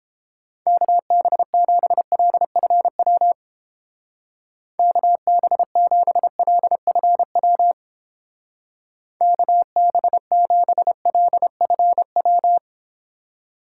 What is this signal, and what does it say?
Morse code, heard as a keyed tone, about 30 wpm, 700 Hz. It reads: KB7LFW KB7LFW KB7LFW